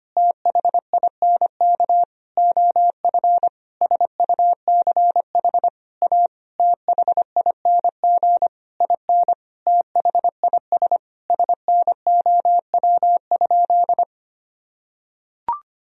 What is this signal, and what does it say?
Morse code, heard as a keyed tone, about 25 wpm, 700 Hz. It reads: THINK OF SUCH A THING IN THIS SNOW?
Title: Sentences from Top 1000 Words - Review Part 3 - 25wpm